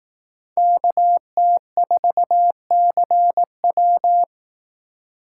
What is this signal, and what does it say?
Morse code, heard as a keyed tone, about 18 wpm, 700 Hz. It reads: KT4CW
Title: Speed Racing - USA 2x2 Call Signs 15wpm